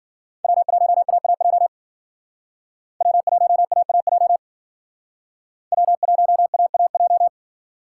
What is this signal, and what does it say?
Morse code, heard as a keyed tone, about 51 wpm, 700 Hz. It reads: W1AAJ W1AAJ W1AAJ